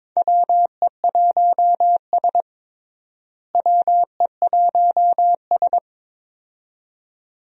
Morse code, heard as a keyed tone, about 22 wpm, 700 Hz. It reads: WE1S WE1S
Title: USA Call Signs - 4 Characters - Repeated 2x - 22wpm